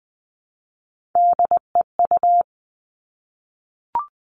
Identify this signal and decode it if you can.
Morse code, heard as a keyed tone, about 20 wpm, 700 Hz. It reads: DEU
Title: Three Challenging Letter-Number Combinations 20wpm